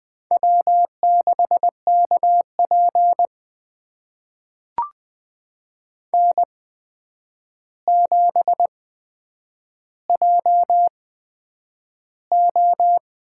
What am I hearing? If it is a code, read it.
Morse code, heard as a keyed tone, about 20 wpm, spaced as for 9 wpm, 700 Hz. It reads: W6KP N7JO
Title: USA Call Signs - 4 Characters - 8x Character Spacing - 20wpm